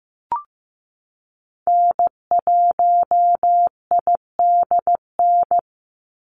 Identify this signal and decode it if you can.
Morse code, heard as a keyed tone, about 15 wpm, 700 Hz. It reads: N1IDN
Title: NA SOTA Activator Callsigns 15wpm